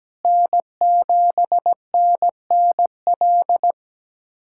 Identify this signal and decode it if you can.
Morse code, heard as a keyed tone, about 17 wpm, 700 Hz. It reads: N7NNL
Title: USA Callsign DE Callsign - 5 Characters - 17wpm